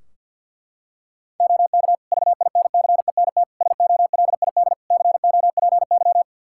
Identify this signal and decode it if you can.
Morse code, heard as a keyed tone, about 50 wpm, 700 Hz. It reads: OK VINCENT SOLID COPY
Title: Practice QSO - Exchange 3 - More Info - 50wpm